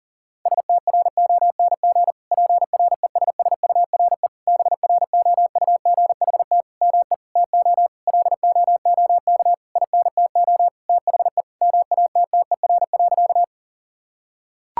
Morse code, heard as a keyed tone, about 40 wpm, 700 Hz. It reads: STRONG PRESSURE BROUGHT ME TO LOOK INTO THE MATTER.
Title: Sentences from Top 1000 Words - Review Part 6 - 40wpm